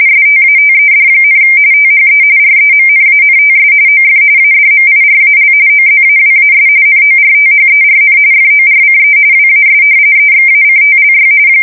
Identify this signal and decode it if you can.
RTTY, about 45.45 baud. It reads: STATED, THE REMNANTS OF ADMIRAL OZAWAS NORTHERN FORCE WAS EXACTLY
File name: fsk170 / battleshipnj